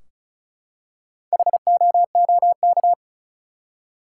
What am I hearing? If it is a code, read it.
Morse code, heard as a keyed tone, about 36 wpm, 700 Hz. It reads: HOOK
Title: Four Letter Words - 35wpm